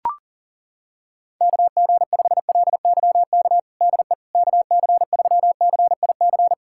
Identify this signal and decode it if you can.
Morse code, heard as a keyed tone, about 40 wpm, 700 Hz. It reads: KG5LYK DE KC3CIC
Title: USA Callsign DE Callsign - 6 Characters - 40wpm